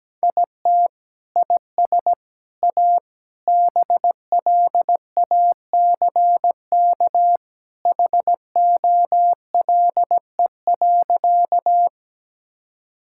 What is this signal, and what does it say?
Morse code, heard as a keyed tone, about 17 wpm, 700 Hz. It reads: IT IS A BLACK HOLE.
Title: Sentences from Top 1600 Words 17wpm